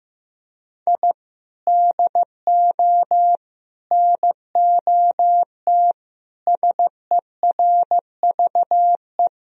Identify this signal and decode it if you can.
Morse code, heard as a keyed tone, about 15 wpm, 700 Hz. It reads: I DO NOT SERVE
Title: Sentences from Top 1000 Words - Review Part 2 - 15wpm